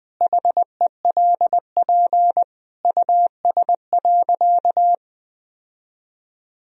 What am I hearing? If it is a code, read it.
Morse code, heard as a keyed tone, about 20 wpm, 700 Hz. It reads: HELP US.
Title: Sentences from Top 1000 Words - Review Part 3 - 20wpm